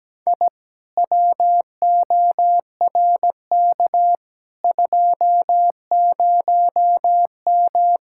Morse code, heard as a keyed tone, about 17 wpm, 700 Hz. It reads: I WORK 20M
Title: Speed-Racing Common QSO Phrases - 17wpm